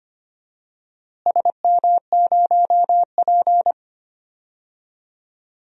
Morse code, heard as a keyed tone, about 25 wpm, 700 Hz. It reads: SM0P
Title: DXCC-100 Call Signs - 4 Characters - Repeated 2x - 25wpm